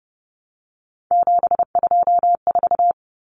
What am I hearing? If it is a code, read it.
Morse code, heard as a keyed tone, about 30 wpm, 700 Hz. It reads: 724
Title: Numbers - 3-Digit 30wpm